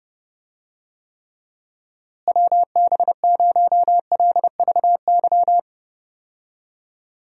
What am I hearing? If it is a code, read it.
Morse code, heard as a keyed tone, about 32 wpm, 700 Hz. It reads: WB0LVY
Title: USA Call Signs - 6 Characters - Repeated 2x - 30wpm